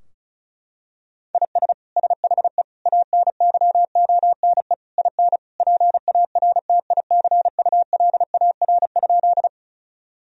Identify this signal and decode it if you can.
Morse code, heard as a keyed tone, about 36 wpm, 700 Hz. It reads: IS SHE ANYONE IN PARTICULAR?
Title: Sentences from Top 1000 Words - Review Part 4 - 35wpm